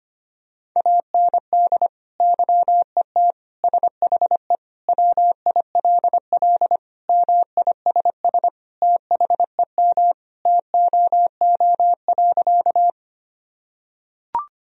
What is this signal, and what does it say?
Morse code, heard as a keyed tone, about 25 wpm, 700 Hz. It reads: AND YET SHE WILL MISS THEM TOO.
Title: Sentences from Top 1000 Words - Review Part 3 - 25wpm